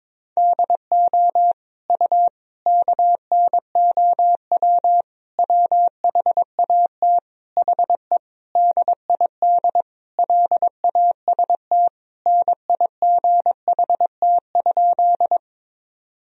Morse code, heard as a keyed tone, about 22 wpm, 700 Hz. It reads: DO U KNOW WHAT HE DID LAST NIGHT?
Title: Sentences from Top 300 Words and Abbreviations 22wpm